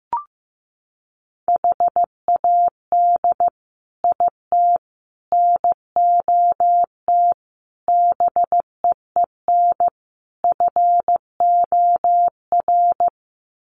Morse code, heard as a keyed tone, about 15 wpm, 700 Hz. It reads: HAD IT NOT BEEN FOR
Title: Sets of 5 Words Encore2 15wpm